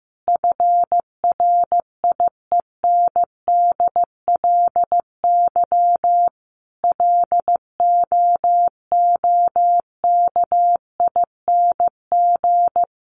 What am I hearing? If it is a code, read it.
Morse code, heard as a keyed tone, about 15 wpm, 700 Hz. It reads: FRIENDLY LOOKING
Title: Easy Phrases 15wpm